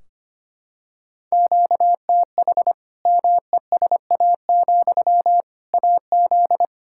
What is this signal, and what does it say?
Morse code, heard as a keyed tone, about 25 wpm, 700 Hz. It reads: QTH MESA, AZ
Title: Practice QSO - Exchange 1 - Beginning - 25wpm